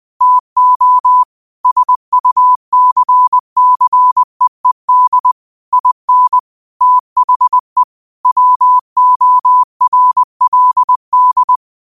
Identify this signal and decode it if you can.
Morse code, heard as a keyed tone, about 20 wpm, 1000 Hz. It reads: TO SUCCEED IN THE WORLD